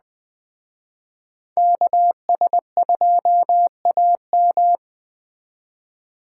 Morse code, heard as a keyed tone, about 20 wpm, 700 Hz. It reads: KS2AM
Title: Speed Racing - USA Call Signs 17wpm